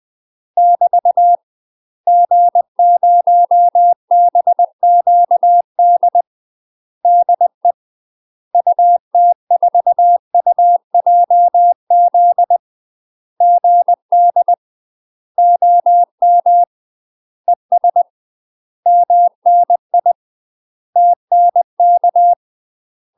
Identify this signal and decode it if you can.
Morse code, heard as a keyed tone, about 20 wpm, 700 Hz. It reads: <BT> G0BQD DE UT4UJZ GD OM ES MNI TNK
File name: UK MASTER QSOs TRAINING / 20WPM / UK MASTER QSOs 0038